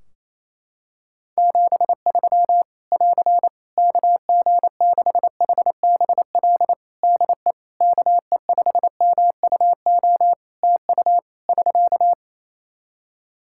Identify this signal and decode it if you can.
Morse code, heard as a keyed tone, about 28 wpm, 700 Hz. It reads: 73 <AR> KG6HBL DE KE5MUO TU <SK>
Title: Practice QSO - Exchange 4 - Ending - 28wpm